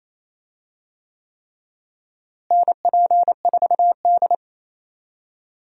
Morse code, heard as a keyed tone, about 28 wpm, 700 Hz. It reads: NP4D
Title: USA Call Signs - 4 Characters - Repeated 3x - 28wpm